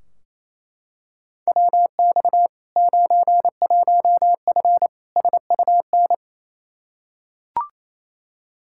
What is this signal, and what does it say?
Morse code, heard as a keyed tone, about 27 wpm, 700 Hz. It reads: WX 91F SUN
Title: Practice QSO - Exchange 2 - Info - 28wpm